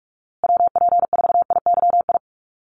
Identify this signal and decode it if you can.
Morse code, heard as a keyed tone, about 44 wpm, 700 Hz. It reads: WP4IYI
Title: USA Call Signs Repeated 3x 45wpm